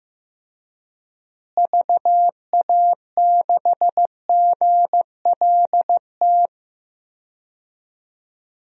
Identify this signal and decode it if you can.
Morse code, heard as a keyed tone, about 15 wpm, 700 Hz. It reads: VA6GLT
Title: DXCC-100 Call Signs - 6 Characters - Repeated 2x - 15wpm